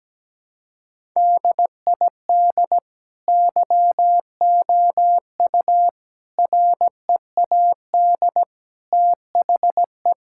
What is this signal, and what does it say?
Morse code, heard as a keyed tone, about 17 wpm, 700 Hz. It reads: DID YOU READ THE
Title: Sentences from Top 1000 Words - Review Part 5 - 17wpm